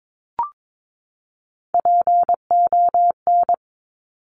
Morse code, heard as a keyed tone, about 22 wpm, 700 Hz. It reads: PON
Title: Top Three Letter Combinations in Words 22wpm